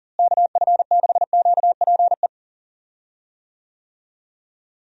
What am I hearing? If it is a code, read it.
Morse code, heard as a keyed tone, about 37 wpm, 700 Hz. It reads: KF6QPE
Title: USA Call Signs - 6 Characters - Repeated 3x - 40wpm